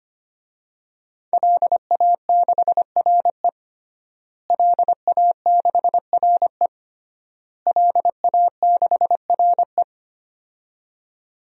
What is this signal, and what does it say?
Morse code, heard as a keyed tone, about 25 wpm, 700 Hz. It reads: LA6RE LA6RE LA6RE
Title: DXCC-100 Call Signs - 5 Characters - Repeated 3x - 25wpm